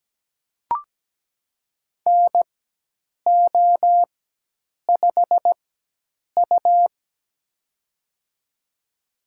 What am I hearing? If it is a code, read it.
Morse code, heard as a keyed tone, about 17 wpm, spaced as for 8 wpm, 700 Hz. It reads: NO5U
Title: USA Call Signs - 4 Characters - 4x Character Spacing - 17wpm